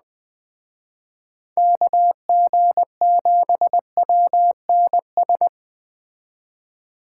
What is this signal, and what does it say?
Morse code, heard as a keyed tone, about 20 wpm, 700 Hz. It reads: KG7WNS
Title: Speed Racing - USA 2x3 Call Signs 17wpm